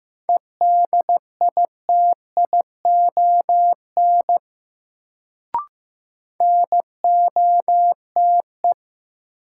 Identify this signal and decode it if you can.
Morse code, heard as a keyed tone, about 15 wpm, 700 Hz. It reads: EDITION NOTE
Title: Sets of 5 Words - New Word Drill - 15wpm